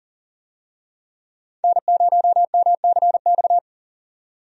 Morse code, heard as a keyed tone, about 39 wpm, 700 Hz. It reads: N0MCX
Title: USA Call Signs - 5 Characters - 40wpm